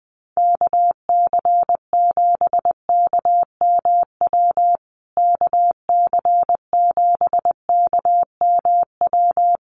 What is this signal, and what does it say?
Morse code, heard as a keyed tone, about 20 wpm, 700 Hz. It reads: KC7KMW KC7KMW
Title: USA Call Signs Repeated 3x 20wpm